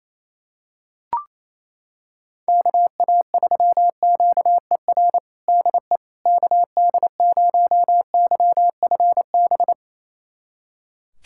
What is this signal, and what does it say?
Morse code, heard as a keyed tone, about 27 wpm, 700 Hz. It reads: KA3QER DE KD0YFB
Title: USA Callsign DE Callsign - 6 Characters - 28wpm